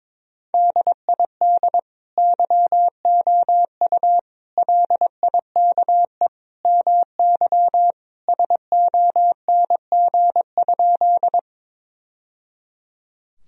Morse code, heard as a keyed tone, about 22 wpm, 700 Hz. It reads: DID YOU LIKE MY SONG?
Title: Sentences from Top 1000 Words - Review Part 2 - 22wpm